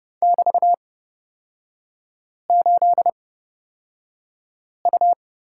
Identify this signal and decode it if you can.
Morse code, heard as a keyed tone, about 30 wpm, spaced as for 5 wpm, 700 Hz. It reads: <BT>8U
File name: Lesson 36e - BT - Three Characters 30wpm